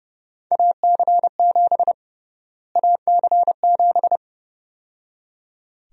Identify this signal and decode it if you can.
Morse code, heard as a keyed tone, about 30 wpm, 700 Hz. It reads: AC7 AC7
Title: USA Call Sign Prefixes - Repeated 2x - 30wpm